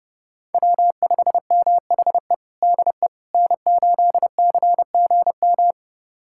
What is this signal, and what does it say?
Morse code, heard as a keyed tone, about 29 wpm, 700 Hz. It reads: W5MHE DE N8CGM
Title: USA Callsign DE Callsign - 5 Characters - 30wpm